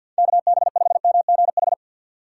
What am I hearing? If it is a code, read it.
Morse code, heard as a keyed tone, about 51 wpm, 700 Hz. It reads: KB5MGH
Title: USA Call Signs - 4x Character Spacing - 50wpm